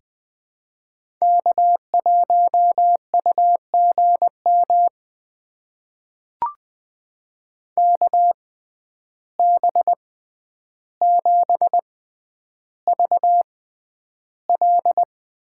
Morse code, heard as a keyed tone, about 20 wpm, spaced as for 10 wpm, 700 Hz. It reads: K1UGM KB7VL
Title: USA Call Signs - 6x Character Spacing - 20wpm